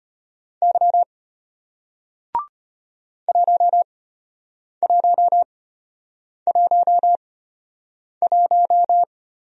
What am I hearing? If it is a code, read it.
Morse code, heard as a keyed tone, about 34 wpm, spaced as for 8 wpm, 700 Hz. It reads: Y 1111